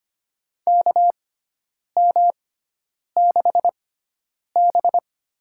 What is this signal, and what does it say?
Morse code, heard as a keyed tone, about 25 wpm, spaced as for 9 wpm, 700 Hz. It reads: KM6B
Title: USA Call Signs - 6x Character Spacing - 25wpm